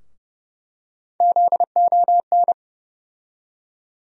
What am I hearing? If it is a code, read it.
Morse code, heard as a keyed tone, about 30 wpm, 700 Hz. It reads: ZON
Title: USA Call Sign Postfixes - 3 Letters 30wpm